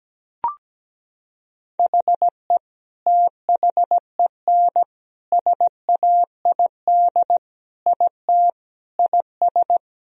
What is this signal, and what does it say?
Morse code, heard as a keyed tone, about 17 wpm, 700 Hz. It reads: HE THEN SAID IT IS
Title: Sentences from Top 1000 Words - Review Part 1 - 17wpm